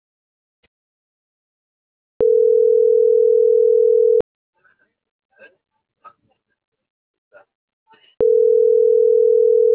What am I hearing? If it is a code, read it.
Ringback tone (North America), 440 + 480 Hz.